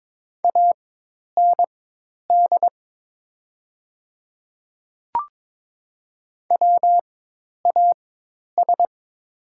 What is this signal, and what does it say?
Morse code, heard as a keyed tone, about 22 wpm, spaced as for 10 wpm, 700 Hz. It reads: AND WAS
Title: 4x Character Spacing - Top 100 Words - 22wpm